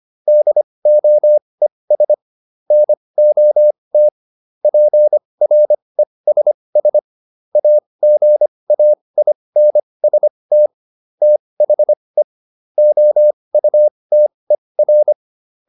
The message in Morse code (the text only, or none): DOES NOT PRESS AGAINST THE OUTER